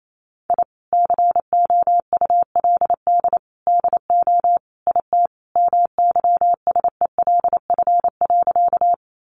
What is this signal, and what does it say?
Morse code, heard as a keyed tone, about 28 wpm, 700 Hz. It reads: I COULD DO IT MYSELF.
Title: Sentences from Top 1300 Words 28wpm